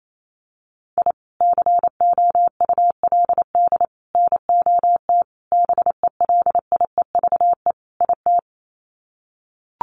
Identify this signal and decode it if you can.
Morse code, heard as a keyed tone, about 28 wpm, 700 Hz. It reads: I COULD NOT BELIEVE IT
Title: Sets of 5 Words Encore 28wpm